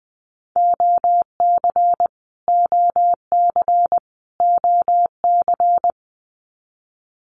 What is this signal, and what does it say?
Morse code, heard as a keyed tone, about 20 wpm, 700 Hz. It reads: OC OC OC
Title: Abbreviations Repeated 3x - 20wpm